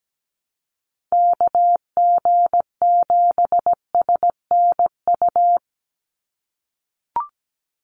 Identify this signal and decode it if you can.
Morse code, heard as a keyed tone, about 17 wpm, 700 Hz. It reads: KG7SNU
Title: USA Call Signs 17wpm